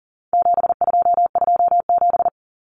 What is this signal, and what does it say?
Morse code, heard as a keyed tone, about 40 wpm, 700 Hz. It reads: 7227
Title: Common Cut Numbers - 4-Digit 40wpm